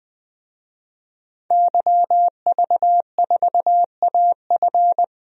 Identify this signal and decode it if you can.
Morse code, heard as a keyed tone, about 20 wpm, 700 Hz. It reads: YV4AF
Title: DXCC-100 Call Signs - 5 Characters - 20wpm